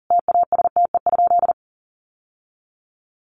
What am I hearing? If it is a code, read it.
Morse code, heard as a keyed tone, about 40 wpm, 700 Hz. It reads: TASTE?
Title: Sentences from Top 1100 Words 40wpm